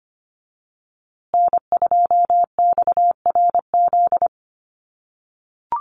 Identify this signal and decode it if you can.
Morse code, heard as a keyed tone, about 25 wpm, 700 Hz. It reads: N2XRZ
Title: USA Call Signs Repeated 3x 25wpm